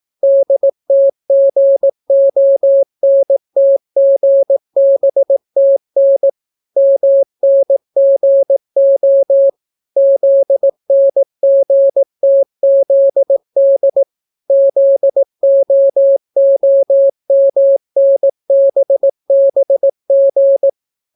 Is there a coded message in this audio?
Yes: Morse code from 0.2 to 20.7 s, heard as a keyed tone, about 18 wpm, 550 Hz. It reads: DTGONTGBTN MNGO ZNGTZD ZOOMNBBG